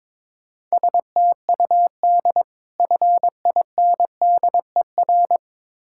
Morse code, heard as a keyed tone, about 22 wpm, 690 Hz. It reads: STUD FINDER
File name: Difficult Words with Hints - 22wpm